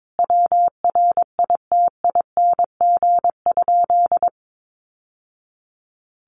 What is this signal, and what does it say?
Morse code, heard as a keyed tone, about 22 wpm, 700 Hz. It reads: WRITING?
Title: Sentences from Top 1200 Words 22wpm